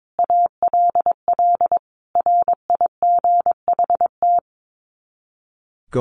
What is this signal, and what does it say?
Morse code, heard as a keyed tone, about 22 wpm, 700 Hz. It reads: ALL RIGHT
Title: Sets of 5 Words Encore2 22wpm